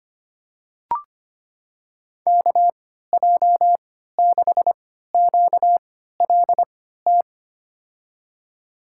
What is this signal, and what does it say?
Morse code, heard as a keyed tone, about 25 wpm, spaced as for 14 wpm, 700 Hz. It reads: KJ6QLT
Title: USA Call Signs - 6 Characters - 3x Character Spacing - 25wpm